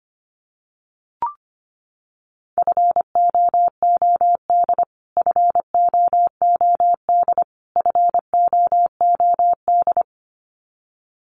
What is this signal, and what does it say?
Morse code, heard as a keyed tone, about 25 wpm, 700 Hz. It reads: FOOD FOOD FOOD
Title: Top 500 Words Repeated 25wpm